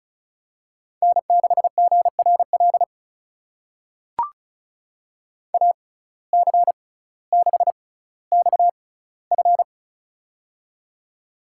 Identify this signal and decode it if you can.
Morse code, heard as a keyed tone, about 36 wpm, spaced as for 18 wpm, 700 Hz. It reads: N6GRL AC6XF
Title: USA Call Signs - 5 Characters - 6x Character Spacing - 35wpm